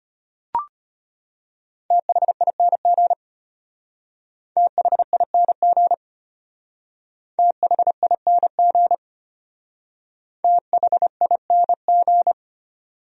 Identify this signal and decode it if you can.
Morse code, heard as a keyed tone, about 32 wpm, 700 Hz. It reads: THING THING THING THING